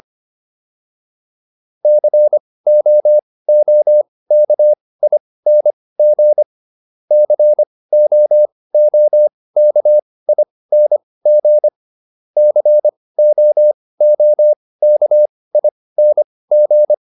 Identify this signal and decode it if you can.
Morse code, heard as a keyed tone, about 25 wpm, spaced as for 18 wpm, 600 Hz. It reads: COOKING COOKING COOKING